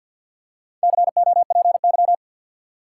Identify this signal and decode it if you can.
Morse code, heard as a keyed tone, about 53 wpm, 700 Hz. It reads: COPY
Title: Four Letter Words - 50wpm